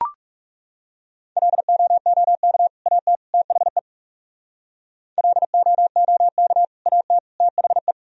Morse code, heard as a keyed tone, about 40 wpm, 700 Hz. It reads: LOOK AT THE LOOK AT THE